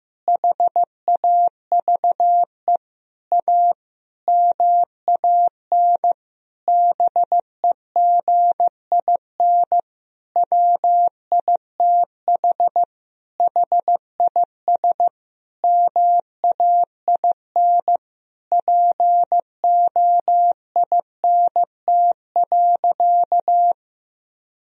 Morse code, heard as a keyed tone, about 15 wpm, 700 Hz. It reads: HAVE A MAN BEGIN WITH HIS MAIN POINT.
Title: Sentences from Top 1000 Words - Review Part 2 - 15wpm